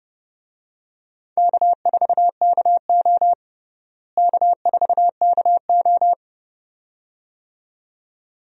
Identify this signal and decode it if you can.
Morse code, heard as a keyed tone, about 30 wpm, 700 Hz. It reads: K4KO K4KO
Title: USA Call Signs - 4 Characters - Repeated 2x - 30wpm